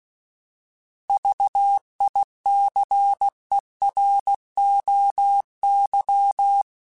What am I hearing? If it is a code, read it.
Morse code, heard as a keyed tone, about 16 wpm, 800 Hz. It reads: VICEROY